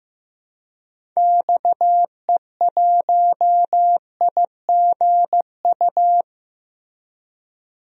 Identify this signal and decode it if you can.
Morse code, heard as a keyed tone, about 15 wpm, 700 Hz. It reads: XE1IGU
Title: DXCC-100 Call Signs - 6 Characters - Repeated 3x - 15wpm